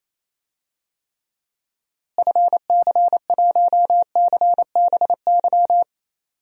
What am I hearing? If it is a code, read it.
Morse code, heard as a keyed tone, about 28 wpm, 700 Hz. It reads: FC1CBY
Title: DXCC-100 Call Signs - 6 Characters - Repeated 2x - 28wpm